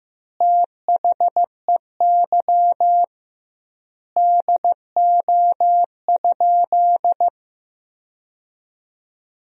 Morse code, heard as a keyed tone, about 16 wpm, 700 Hz. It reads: THEY DO?
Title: 2x Word-Spacing - Sentences from Top 100 Words - 15wpm